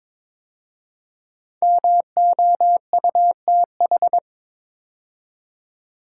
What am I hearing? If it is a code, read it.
Morse code, heard as a keyed tone, about 22 wpm, 690 Hz. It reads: MOUTH
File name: 5 Letter Words - First 3 Letters Given - 22wpm